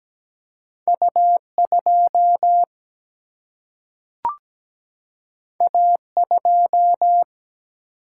Morse code, heard as a keyed tone, about 17 wpm, 700 Hz. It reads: U 2 A 2
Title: Single-Letter Single-Number 17wpm